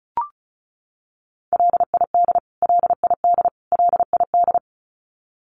Morse code, heard as a keyed tone, about 34 wpm, 700 Hz. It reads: LID LID LID
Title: Abbreviations Repeated 3x - 35wpm